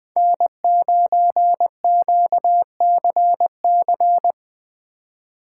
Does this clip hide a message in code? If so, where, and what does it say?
Yes: Morse code from 0.2 to 4.3 s, heard as a keyed tone, about 19 wpm, 700 Hz. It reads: N9QCC